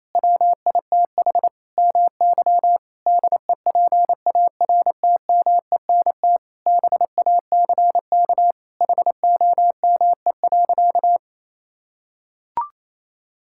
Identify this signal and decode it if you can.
Morse code, heard as a keyed tone, about 28 wpm, 700 Hz. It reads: WITH MY DEPARTMENT BACK HOME.
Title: Sentences from Top 1000 Words - Review Part 6 - 28wpm